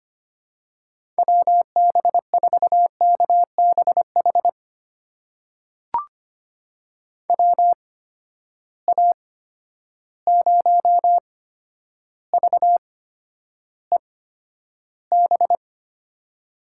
Morse code, heard as a keyed tone, about 25 wpm, spaced as for 11 wpm, 700 Hz. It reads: WB4KBH WA0VEB
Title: USA Call Signs - 6 Characters - 8x Character Spacing - 25wpm